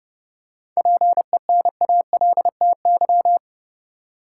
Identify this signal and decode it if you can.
Morse code, heard as a keyed tone, about 29 wpm, 700 Hz. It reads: PENALTY
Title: Seven Letter Words - 30wpm